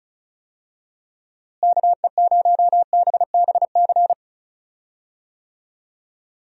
Morse code, heard as a keyed tone, about 36 wpm, 700 Hz. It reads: KE0BBC
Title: USA Call Signs - 6 Characters - Repeated 2x - 35wpm